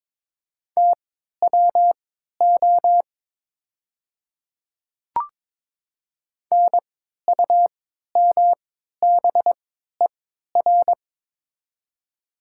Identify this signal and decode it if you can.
Morse code, heard as a keyed tone, about 22 wpm, spaced as for 13 wpm, 700 Hz. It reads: TWO NUMBER
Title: 3x Character Spacing - Top 100 Words - 22wpm